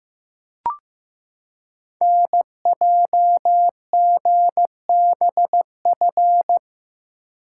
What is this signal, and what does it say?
Morse code, heard as a keyed tone, about 15 wpm, 700 Hz. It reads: NJGBF
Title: Code Groups - 5 Letters - 15wpm